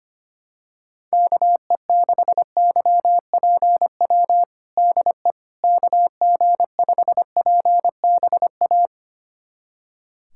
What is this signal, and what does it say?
Morse code, heard as a keyed tone, about 25 wpm, 700 Hz. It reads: KE6YPW DE KG5PBA
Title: USA Callsign DE Callsign - 6 Characters - 25wpm